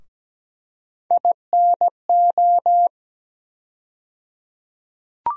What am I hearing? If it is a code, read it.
Morse code, heard as a keyed tone, about 17 wpm, 700 Hz. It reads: INO
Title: USA Call Sign Postfixes - 3 Letters 17wpm